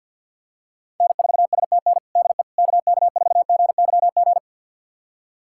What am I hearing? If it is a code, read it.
Morse code, heard as a keyed tone, about 51 wpm, 700 Hz. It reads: N4STN DE KK4GYG
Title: USA Callsign DE Callsign 50wpm